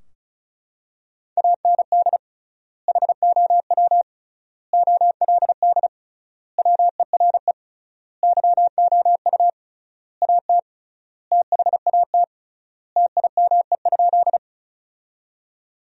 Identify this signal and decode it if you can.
Morse code, heard as a keyed tone, about 36 wpm, 700 Hz. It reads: AND HOW OLD WERE YOU AT THAT TIME?
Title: 3x Word-Spacing - Sentences from Top 200 Words - 35wpm